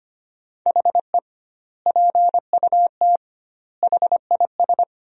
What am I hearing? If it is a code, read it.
Morse code, heard as a keyed tone, about 25 wpm, 700 Hz. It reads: HE PUT HIS
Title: 2x Word-Spacing - Sentences from Top 400 Words - 25wpm